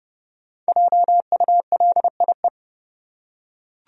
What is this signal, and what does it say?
Morse code, heard as a keyed tone, about 32 wpm, 700 Hz. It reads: JULIE
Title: USA Common Names - 30wpm